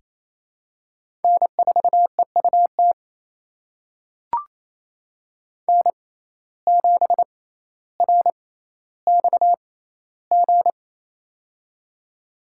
Morse code, heard as a keyed tone, about 28 wpm, spaced as for 14 wpm, 700 Hz. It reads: N4EUT N7RXG
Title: USA Call Signs - 5 Characters - 6x Character Spacing - 28wpm